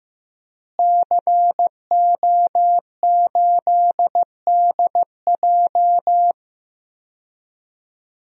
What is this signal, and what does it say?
Morse code, heard as a keyed tone, about 15 wpm, 700 Hz. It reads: CO8DJ
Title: DXCC-100 Call Signs - 5 Characters - Repeated 3x - 15wpm